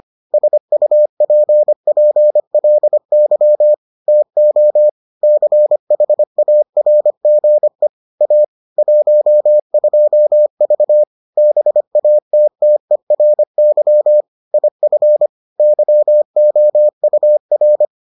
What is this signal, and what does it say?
Morse code, heard as a keyed tone, about 25 wpm, 600 Hz. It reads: SUPPLY TO CHARGE A 12V BATTERY IF YOUR